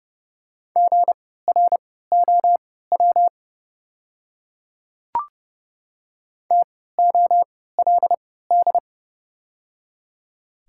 Morse code, heard as a keyed tone, about 31 wpm, spaced as for 17 wpm, 700 Hz. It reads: GROW TOLD
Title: 3x Character Spacing - Top 500 Words - 30wpm